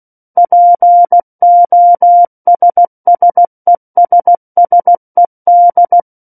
Morse code, heard as a keyed tone, about 16 wpm, 700 Hz. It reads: POSSESSED
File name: Book One / 07-Chapter 7 - 16 WPM